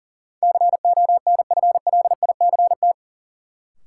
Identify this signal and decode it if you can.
Morse code, heard as a keyed tone, about 40 wpm, 700 Hz. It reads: CONFLICT
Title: Sentences from Top 1300 Words - New Word Drill - 40wpm